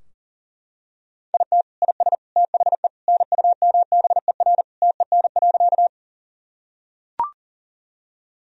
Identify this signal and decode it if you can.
Morse code, heard as a keyed tone, about 40 wpm, 700 Hz. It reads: IT IS THE NUMBER TEN.
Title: Sentences from Top 1000 Words - Review Part 2 - 40wpm